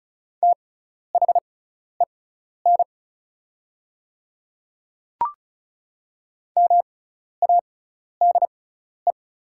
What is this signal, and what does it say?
Morse code, heard as a keyed tone, about 36 wpm, spaced as for 12 wpm, 700 Hz. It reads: THEN MADE
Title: 6x Character Spacing - Top 200 Words - 35wpm